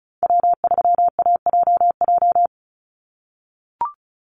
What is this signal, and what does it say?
Morse code, heard as a keyed tone, about 34 wpm, 700 Hz. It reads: W3AJJ
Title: USA Call Signs 35wpm